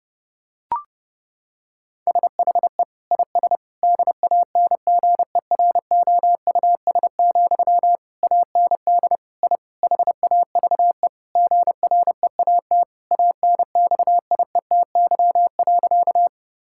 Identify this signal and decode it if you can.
Morse code, heard as a keyed tone, about 30 wpm, 700 Hz. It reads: SHE IS DANGEROUS, AND I HAVE GREAT ANXIETY.
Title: Sentences from Top 1600 Words 30wpm